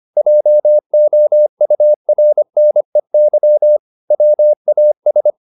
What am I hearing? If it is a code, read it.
Morse code, heard as a keyed tone, about 25 wpm, 600 Hz. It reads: JOURNEY WAS